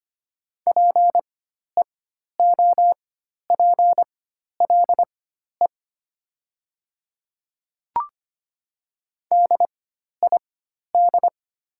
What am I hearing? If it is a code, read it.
Morse code, heard as a keyed tone, about 25 wpm, spaced as for 12 wpm, 700 Hz. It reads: PEOPLE DID
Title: 4x Character Spacing - Top 100 Words - 25wpm